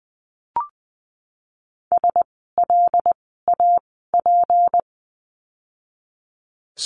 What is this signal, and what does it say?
Morse code, heard as a keyed tone, about 20 wpm, spaced as for 15 wpm, 700 Hz. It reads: SLAP
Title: 153 Lesson 25f - Letter p - Words 20wpm